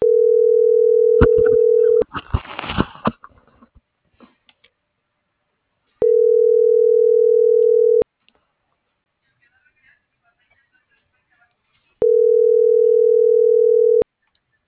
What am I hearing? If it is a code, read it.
Ringback tone (North America), 440 + 480 Hz.